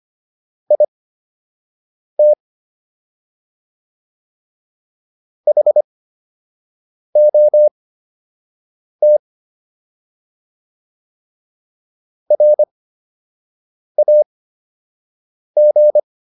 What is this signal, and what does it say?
Morse code, heard as a keyed tone, about 25 wpm, spaced as for 6 wpm, 600 Hz. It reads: IT HOT RAG